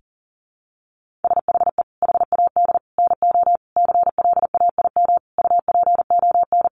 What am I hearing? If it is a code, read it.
Morse code, heard as a keyed tone, about 40 wpm, 700 Hz. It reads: SHE HAD NO CLAIM UPON
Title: Sentences from Top 1500 Words 40wpm